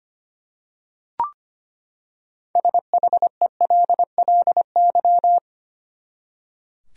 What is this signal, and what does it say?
Morse code, heard as a keyed tone, about 25 wpm, 700 Hz. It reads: SHELLY
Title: USA Common Names - 25wpm